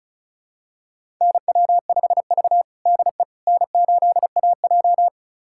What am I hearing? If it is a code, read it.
Morse code, heard as a keyed tone, about 36 wpm, 700 Hz. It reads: NW5V DE N8AJ
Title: USA Callsign DE Callsign - 4 Characters - 35wpm